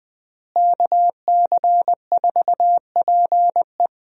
Morse code, heard as a keyed tone, about 21 wpm, 700 Hz. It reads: KC4PE
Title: USA Call Signs - 5 Characters - Repeated 2x - 20wpm